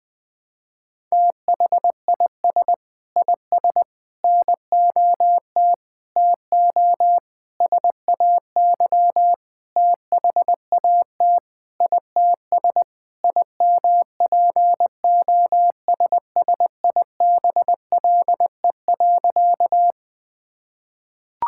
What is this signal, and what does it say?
Morse code, heard as a keyed tone, about 21 wpm, 700 Hz. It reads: THIS IS NOT TO SAY THAT ITS IMPOSSIBLE.
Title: Sentences with 5-Word Phrases - 20wpm